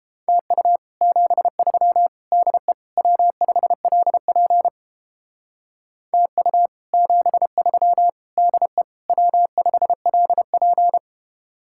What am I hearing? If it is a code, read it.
Morse code, heard as a keyed tone, about 31 wpm, 700 Hz. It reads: TU 73 DE W5LP TU 73 DE W5LP